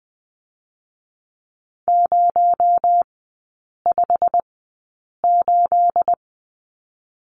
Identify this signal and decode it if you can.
Morse code, heard as a keyed tone, about 20 wpm, spaced as for 8 wpm, 700 Hz. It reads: 058
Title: 218 Lesson 35e - Number 0 - Three Characters 20wpm